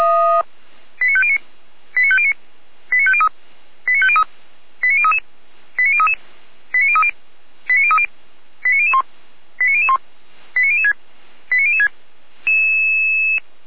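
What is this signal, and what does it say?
Five-tone selective call (ZVEI), heard as quick runs of five tones, tones of about 70 ms. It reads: 89509 89509 89503 89503 89033 89033 89030 89030 89001 89001 89006 89006
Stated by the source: ZVEI five-tone selective call 89099 received on SDR.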